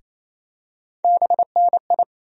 Morse code, heard as a keyed tone, about 28 wpm, 700 Hz. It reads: BNI
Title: USA Call Sign Postfixes - 3 Letters 28wpm